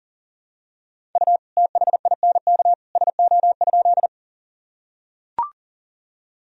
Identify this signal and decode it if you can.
Morse code, heard as a keyed tone, about 40 wpm, 700 Hz. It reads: U THINK SO?
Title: Sentences from Top 200 Words and Abbreviations 40wpm